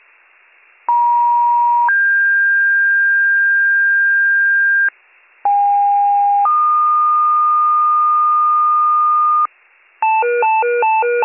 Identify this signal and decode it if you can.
Two-tone page, heard as a one-second tone, then a three-second tone, about 950 then 1600 Hz, about 810 then 1220 Hz, with noise. It is heard from 0.9 to 9.5 s.